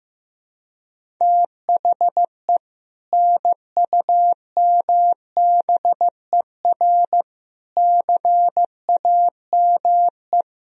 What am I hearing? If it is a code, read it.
Morse code, heard as a keyed tone, about 15 wpm, 700 Hz. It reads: THE NUMBER CAME
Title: Sentences from Top 1000 Words - Review Part 1 - 15wpm